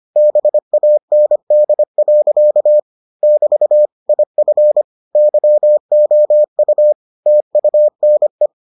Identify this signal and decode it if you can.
Morse code, heard as a keyed tone, about 25 wpm, 600 Hz. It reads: BAND. <BT> IF YOU TUNE